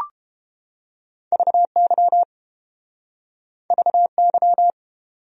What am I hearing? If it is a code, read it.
Morse code, heard as a keyed tone, about 30 wpm, 700 Hz. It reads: VY VY